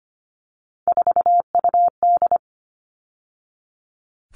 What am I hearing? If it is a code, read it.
Morse code, heard as a keyed tone, about 25 wpm, 700 Hz. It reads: 4UD
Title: All Cut Numbers - 3-Digit 25wpm